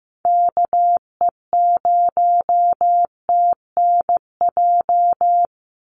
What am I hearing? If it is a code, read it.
Morse code, heard as a keyed tone, about 15 wpm, 700 Hz. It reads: KE0TNJ
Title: USA Call Signs 15wpm